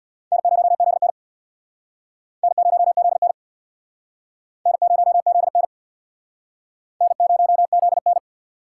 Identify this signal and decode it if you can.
Morse code, heard as a keyed tone, about 61 wpm, 700 Hz. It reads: N0ZN N0ZN N0ZN N0ZN
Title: Speed Racing - USA 1x2 Call Signs 50wpm